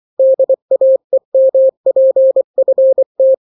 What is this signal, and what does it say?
Morse code, heard as a keyed tone, about 24 wpm, 530 Hz. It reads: DAEMPFT
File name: daily / 23wpm